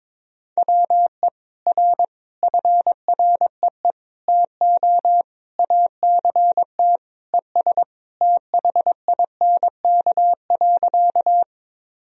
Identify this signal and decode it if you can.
Morse code, heard as a keyed tone, about 22 wpm, 700 Hz. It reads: WE R FREE TO ACT ES THINK.
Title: Sentences from Top 500 Words and Abbreviations 22wpm